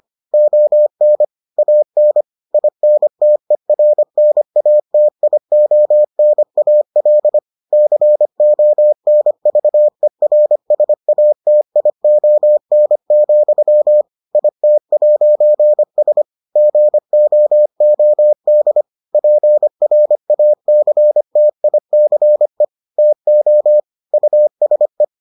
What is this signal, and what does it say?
Morse code, heard as a keyed tone, about 25 wpm, 600 Hz. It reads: ON AN INTERNATIONAL CONVERSATION, IT'S GOOD PRACTICE TO USE